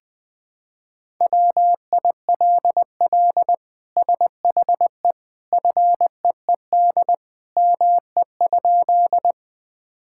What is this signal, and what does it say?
Morse code, heard as a keyed tone, about 20 wpm, 700 Hz. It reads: WILL SHE FEED ME?
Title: Sentences from Top 1600 Words 20wpm